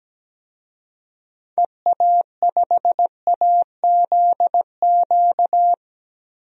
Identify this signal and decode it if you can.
Morse code, heard as a keyed tone, about 17 wpm, 700 Hz. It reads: EA5AZQ
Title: DXCC-100 Call Signs - 6 Characters - Repeated 3x - 17wpm